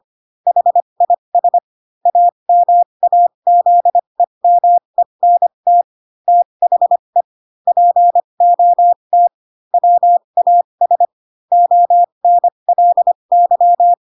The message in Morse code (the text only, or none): HIS AMAZEMENT THE POT WAS ONLY